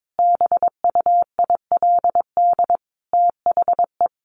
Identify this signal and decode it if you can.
Morse code, heard as a keyed tone, about 22 wpm, 700 Hz. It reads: BUILD THE
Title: Sentences from Top 1500 Words 22wpm